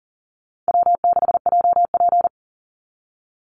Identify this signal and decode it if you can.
Morse code, heard as a keyed tone, about 38 wpm, 700 Hz. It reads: W6JP
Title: NA SOTA Chaser Callsigns 40wpm